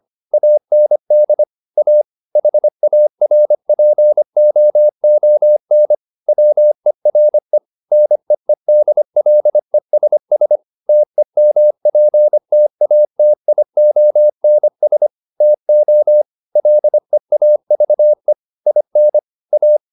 Morse code, heard as a keyed tone, about 25 wpm, 600 Hz. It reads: AND A HARPOON WERE NEEDLESS TEMPTATIONS TO LEAVE IN A
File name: the old man and the sea part 1 25wpm